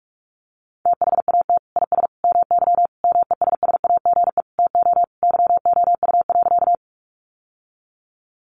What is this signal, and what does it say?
Morse code, heard as a keyed tone, about 44 wpm, 700 Hz. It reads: THAT IS MY MESSAGE TO YOU.
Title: Sentences from Top 1100 Words 45wpm